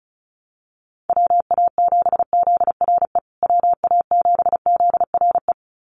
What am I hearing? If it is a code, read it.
Morse code, heard as a keyed tone, about 34 wpm, 700 Hz. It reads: WA7ZRE WA7ZRE
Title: USA Call Signs Repeated 2x 35wpm